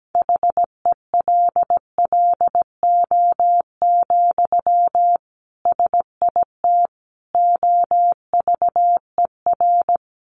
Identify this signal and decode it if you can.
Morse code, heard as a keyed tone, about 17 wpm, 700 Hz. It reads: HELLO, SIT OVER
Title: Sentences from Top 1100 Words 17wpm